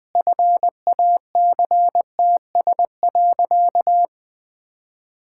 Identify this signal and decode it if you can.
Morse code, heard as a keyed tone, about 19 wpm, 700 Hz. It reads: FACTS.
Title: Sentences with 8-Word Phrases - 20wpm